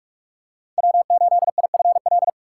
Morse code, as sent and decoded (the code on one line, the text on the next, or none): .-- ---.. .. ..-. .-..
W8IFL